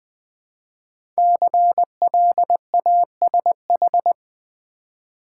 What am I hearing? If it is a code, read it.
Morse code, heard as a keyed tone, about 20 wpm, 700 Hz. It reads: CLASH
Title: 5 Letter Words - First 3 Letters Given - 20wpm